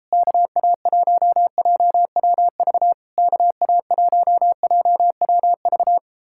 Morse code, heard as a keyed tone, about 38 wpm, 700 Hz. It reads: KA1JWV KA1JWV